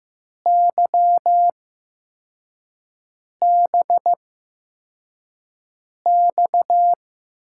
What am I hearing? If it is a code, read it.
Morse code, heard as a keyed tone, about 15 wpm, spaced as for 4 wpm, 700 Hz. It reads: YBX
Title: USA Call Signs - 8x Character Spacing - 15wpm